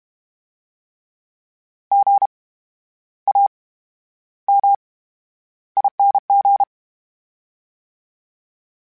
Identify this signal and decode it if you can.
Morse code, heard as a keyed tone, about 32 wpm, spaced as for 11 wpm, 800 Hz. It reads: GAMING
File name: Special INGs / Special INGs-mp3s / VerySlowToFaster Ing @30